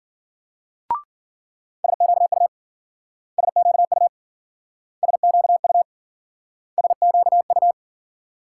Morse code, heard as a keyed tone, about 52 wpm, 700 Hz. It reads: SQU SQU SQU SQU